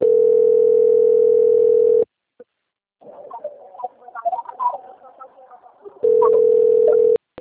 Ringback tone (North America), 440 + 480 Hz.